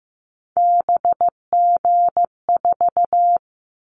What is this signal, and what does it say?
Morse code, heard as a keyed tone, about 15 wpm, 700 Hz. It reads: BG4
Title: Callsign Prefixes for Top 100 DXCC - 15wpm